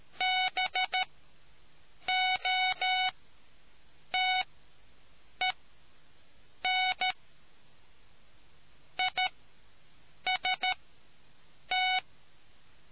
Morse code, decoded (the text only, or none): BOTEN IST